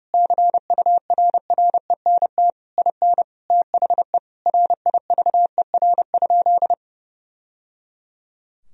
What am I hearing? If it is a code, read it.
Morse code, heard as a keyed tone, about 30 wpm, 700 Hz. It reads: CURRENT IN THE RIVER?
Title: Sentences from Top 1000 Words - Review Part 4 - 30wpm